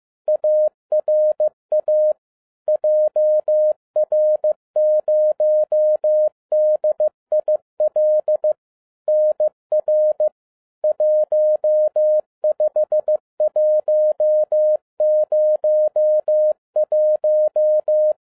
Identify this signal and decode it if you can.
Morse code, heard as a keyed tone, about 15 wpm, 610 Hz. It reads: ARA JR0DIL NR 15101
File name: 15wpm